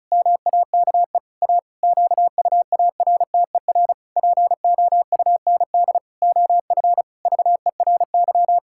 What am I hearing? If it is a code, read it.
Morse code, heard as a keyed tone, about 34 wpm, 700 Hz. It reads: MAKE A QUARTER POUND OF VERY
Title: Pangram Challenge v2 - 35wpm